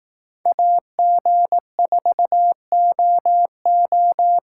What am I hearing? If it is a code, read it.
Morse code, heard as a keyed tone, about 18 wpm, 700 Hz. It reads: AG4OO
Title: Speed Racing - USA 2x2 Call Signs 15wpm